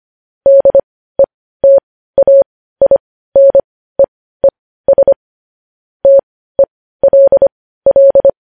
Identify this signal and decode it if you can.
Morse code, heard as a keyed tone, about 25 wpm, spaced as for 15 wpm, 550 Hz. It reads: DETAINEES TELL
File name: podcast / News headlines-1x-25x15